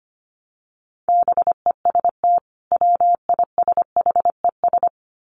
Morse code, heard as a keyed tone, about 25 wpm, 700 Hz. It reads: BEST WISHES
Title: Common QSO Phrases 25wpm